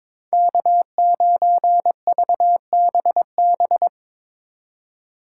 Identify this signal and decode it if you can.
Morse code, heard as a keyed tone, about 24 wpm, 700 Hz. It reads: K9VBB
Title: Speed Racing - USA 1x3 Call Signs 22wpm